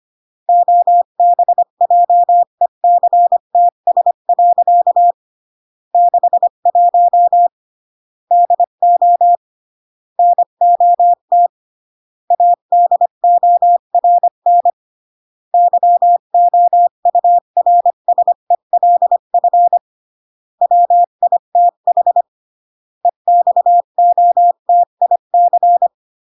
Morse code, heard as a keyed tone, about 25 wpm, 700 Hz. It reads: OBJECTS. 61 DO NOT ADORN YOURSELF WITH EXOTIC